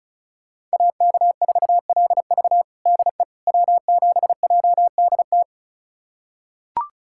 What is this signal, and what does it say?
Morse code, heard as a keyed tone, about 34 wpm, 700 Hz. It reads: AK4LV DE W7JDT
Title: USA Callsign DE Callsign - 5 Characters - 35wpm